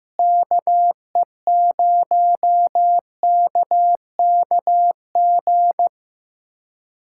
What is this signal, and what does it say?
Morse code, heard as a keyed tone, about 15 wpm, 700 Hz. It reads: KE0KKG
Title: USA Callsign DE Callsign - 6 Characters - 15wpm